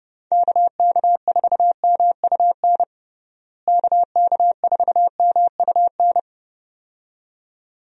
Morse code, heard as a keyed tone, about 30 wpm, 700 Hz. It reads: KK4MUN KK4MUN
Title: USA Call Signs - 6 Characters - Repeated 2x - 30wpm